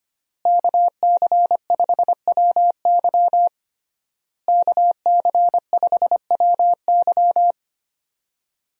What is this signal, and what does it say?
Morse code, heard as a keyed tone, about 25 wpm, 700 Hz. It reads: KC5WY KC5WY
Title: USA Call Signs - 5 Characters - Repeated 3x - 25wpm